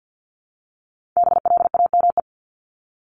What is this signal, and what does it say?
Morse code, heard as a keyed tone, about 49 wpm, 700 Hz. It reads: BLAME
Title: Top 1751-2000 Words 50wpm